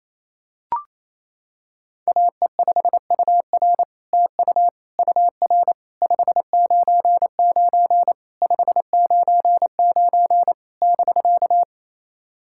Morse code, heard as a keyed tone, about 27 wpm, 700 Hz. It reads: AE5UR TU UR 599 599 <BK>
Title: SOTA QSO Practice 28wpm